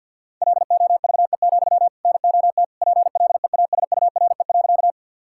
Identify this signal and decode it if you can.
Morse code, heard as a keyed tone, about 52 wpm, 700 Hz. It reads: LOVE, NOT PLEASURE.
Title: Sentences from Top 1600 Words 50wpm